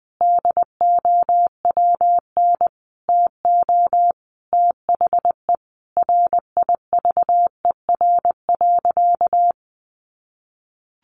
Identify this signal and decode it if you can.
Morse code, heard as a keyed tone, about 20 wpm, 700 Hz. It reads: DOWN TO THE RIVER.
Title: Sentences from Top 1500 Words 20wpm